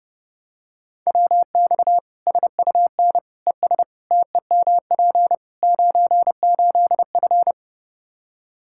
Morse code, heard as a keyed tone, about 32 wpm, 700 Hz. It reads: WX SUN ES TEMP 98F
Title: K1USN SST QSO Practice 30wpm